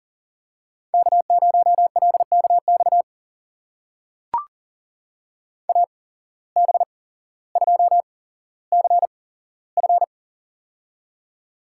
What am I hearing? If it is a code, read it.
Morse code, heard as a keyed tone, about 40 wpm, spaced as for 17 wpm, 700 Hz. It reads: K0LKX AB2CF
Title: USA Call Signs - 5 Characters - 8x Character Spacing - 40wpm